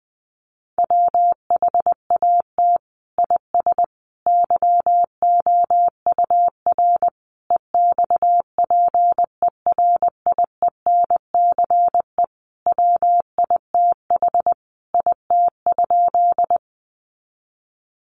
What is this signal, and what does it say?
Morse code, heard as a keyed tone, about 20 wpm, 700 Hz. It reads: WHAT IS YOUR EXPERIENCE WITH IT?
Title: Sentences from Top 600 Words 20wpm